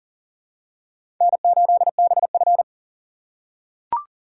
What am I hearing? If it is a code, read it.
Morse code, heard as a keyed tone, about 40 wpm, 700 Hz. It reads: N8BF
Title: USA Call Signs - 4 Characters - 3x Character Spacing - 40wpm